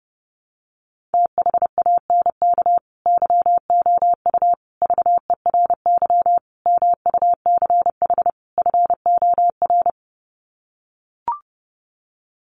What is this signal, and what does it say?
Morse code, heard as a keyed tone, about 30 wpm, 700 Hz. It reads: THANK YOU VERY MUCH FOR
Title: Sets of 5 Words 30wpm